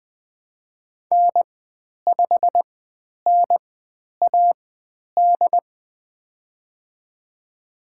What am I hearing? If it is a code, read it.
Morse code, heard as a keyed tone, about 20 wpm, spaced as for 10 wpm, 700 Hz. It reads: N5NAD
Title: 102 Lesson 17g - Letter u - Call Signs 20wpm